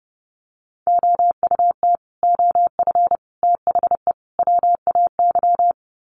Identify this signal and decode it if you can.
Morse code, heard as a keyed tone, about 30 wpm, 700 Hz. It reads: OUT OF THE WAY
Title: Sets of 4 Words 30wpm